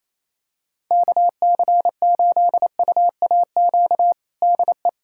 Morse code, heard as a keyed tone, about 28 wpm, 700 Hz. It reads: KC8UAQ DE
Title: USA Callsign DE Callsign - 6 Characters - 28wpm